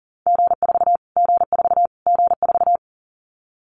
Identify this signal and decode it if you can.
Morse code, heard as a keyed tone, about 40 wpm, 700 Hz. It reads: G4 G4 G4
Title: Callsign Prefixes for Top 25 DXCC - Repeated 3x - 40wpm